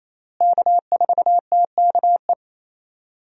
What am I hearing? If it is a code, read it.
Morse code, heard as a keyed tone, about 28 wpm, 700 Hz. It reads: K4TKE
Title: USA Call Signs - 5 Characters - Repeated 3x - 28wpm